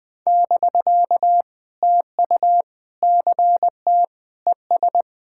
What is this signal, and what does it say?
Morse code, heard as a keyed tone, about 21 wpm, 700 Hz. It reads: <BK> TU CT ES
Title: POTA QSO Practice - 20wpm